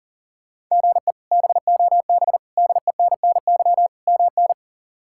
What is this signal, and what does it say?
Morse code, heard as a keyed tone, about 40 wpm, 700 Hz. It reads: GE BOB DENNY MN
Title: K1USN SST QSO Practice - 40wpm